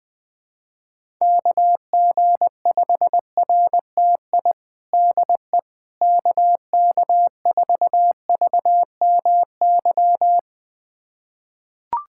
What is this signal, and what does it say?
Morse code, heard as a keyed tone, about 20 wpm, 700 Hz. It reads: KG5RTI DE KK4VMY
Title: USA Callsign DE Callsign 20wpm